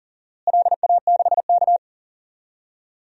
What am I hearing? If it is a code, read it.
Morse code, heard as a keyed tone, about 40 wpm, 700 Hz. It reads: LA6K
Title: DXCC-100 Call Signs - 4 Characters - Repeated 2x - 40wpm